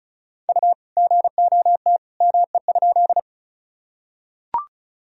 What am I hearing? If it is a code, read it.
Morse code, heard as a keyed tone, about 36 wpm, 700 Hz. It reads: U GOT ME?
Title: Sentences from Top 300 Words and Abbreviations 35wpm